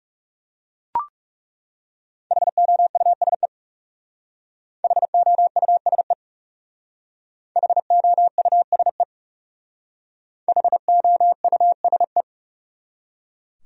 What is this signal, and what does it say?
Morse code, heard as a keyed tone, about 37 wpm, 700 Hz. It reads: HOUSE HOUSE HOUSE HOUSE
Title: Speed-Racing Top 300 Words 30wpm